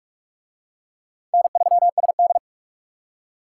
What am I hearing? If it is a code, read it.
Morse code, heard as a keyed tone, about 41 wpm, 700 Hz. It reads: N3SD